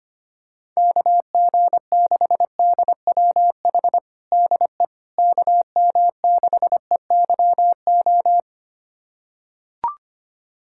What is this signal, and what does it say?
Morse code, heard as a keyed tone, about 25 wpm, 700 Hz. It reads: KG6DWH DE KM6EYO
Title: USA Callsign DE Callsign - 6 Characters - 25wpm